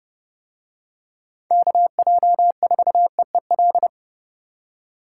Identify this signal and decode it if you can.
Morse code, heard as a keyed tone, about 28 wpm, 700 Hz. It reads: KJ4EEL